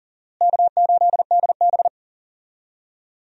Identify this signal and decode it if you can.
Morse code, heard as a keyed tone, about 40 wpm, 700 Hz. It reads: K8DB
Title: USA Call Signs - 4 Characters - Repeated 2x - 40wpm